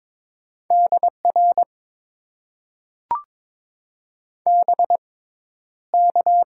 Morse code, heard as a keyed tone, about 22 wpm, spaced as for 11 wpm, 700 Hz. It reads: DR BK
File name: Top 90 QSO Elements - 6x Character Spacing - 22wpm